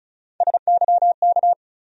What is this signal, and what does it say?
Morse code, heard as a keyed tone, about 36 wpm, 700 Hz. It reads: SYK
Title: USA Call Sign Postfixes - 3 Letters 35wpm